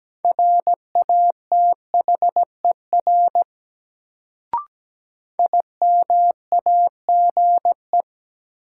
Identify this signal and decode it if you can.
Morse code, heard as a keyed tone, about 17 wpm, 700 Hz. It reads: RATHER IMAGE
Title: Sentences from Top 700 Words - New Word Drill - 17wpm